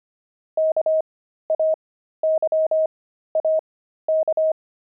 Morse code, heard as a keyed tone, about 25 wpm, spaced as for 13 wpm, 630 Hz. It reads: KAYAK
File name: pangram-fs-13-cs-25-explorer